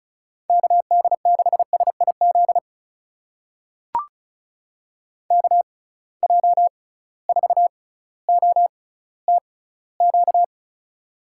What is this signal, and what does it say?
Morse code, heard as a keyed tone, about 36 wpm, spaced as for 18 wpm, 700 Hz. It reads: KD6SIZ KJ4OTQ
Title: USA Call Signs - 6 Characters - 6x Character Spacing - 35wpm